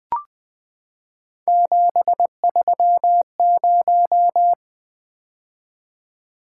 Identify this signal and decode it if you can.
Morse code, heard as a keyed tone, about 20 wpm, 700 Hz. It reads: 730
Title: Three Letter-Number Combinations - 20wpm